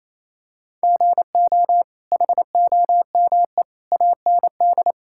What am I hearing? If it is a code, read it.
Morse code, heard as a keyed tone, about 28 wpm, 700 Hz. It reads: GO HOME AND
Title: Sentences from Top 1600 Words 28wpm